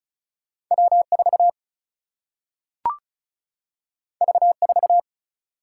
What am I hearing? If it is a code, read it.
Morse code, heard as a keyed tone, about 36 wpm, 700 Hz. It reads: W 4 V 4
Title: Single-Letter Single-Number 35wpm